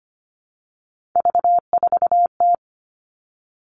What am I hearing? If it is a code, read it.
Morse code, heard as a keyed tone, about 25 wpm, 700 Hz. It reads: V4T
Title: All Cut Numbers - 3-Digit 25wpm